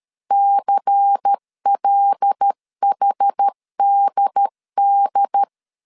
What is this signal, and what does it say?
Morse code, heard as a keyed tone, about 13 wpm, 800 Hz. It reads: CLHDD
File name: CW / DET / 1 6 1500C S C 12wpm 3533